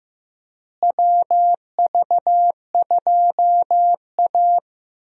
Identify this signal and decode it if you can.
Morse code, heard as a keyed tone, about 15 wpm, 700 Hz. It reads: WV2A
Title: USA Call Signs - 4 Characters - Repeated 2x - 15wpm